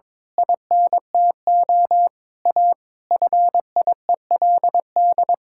Morse code, heard as a keyed tone, about 22 wpm, 700 Hz. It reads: INTO A FIELD
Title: Pangram Challenge v2 - 22wpm